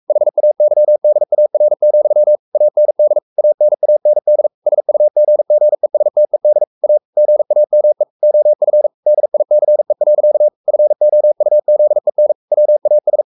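Morse code, heard as a keyed tone, about 44 wpm, 600 Hz. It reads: HAYDAR, AND ANAND SUGGESTED A GAME OF DICE. FOUZEN WAS
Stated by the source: Morse code at 43 wpm sending Ken Follett's novel Never in English, from CW med Gnister.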